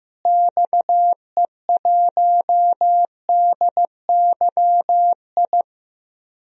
Morse code, heard as a keyed tone, about 15 wpm, 700 Hz. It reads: XE1DYI
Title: DXCC-100 Call Signs - 6 Characters - Repeated 3x - 15wpm